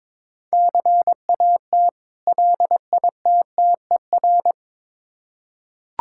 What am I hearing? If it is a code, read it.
Morse code, heard as a keyed tone, about 22 wpm, 690 Hz. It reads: CAT LITTER
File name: Difficult Words with Hints - 22wpm